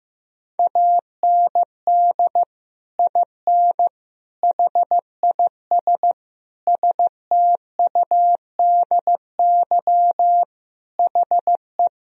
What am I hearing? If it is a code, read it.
Morse code, heard as a keyed tone, about 15 wpm, 700 Hz. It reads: AND IN HIS STUDY HE
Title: Sentences from Top 1000 Words - Review Part 1 - 15wpm